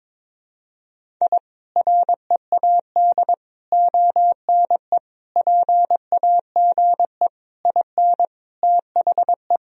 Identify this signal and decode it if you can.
Morse code, heard as a keyed tone, about 22 wpm, 700 Hz. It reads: I READ ONE PAGE IN THE
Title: Sentences from Top 1000 Words - Review Part 1 - 22wpm